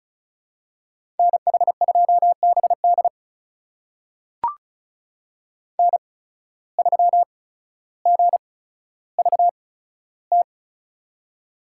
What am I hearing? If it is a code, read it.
Morse code, heard as a keyed tone, about 36 wpm, spaced as for 15 wpm, 700 Hz. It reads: NH2BD N3GVT
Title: USA Call Signs - 5 Characters - 8x Character Spacing - 35wpm